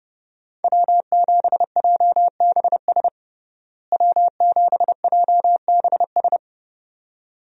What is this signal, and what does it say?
Morse code, heard as a keyed tone, about 32 wpm, 700 Hz. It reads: W7JBS W7JBS
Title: USA Call Signs - 5 Characters - Repeated 2x - 30wpm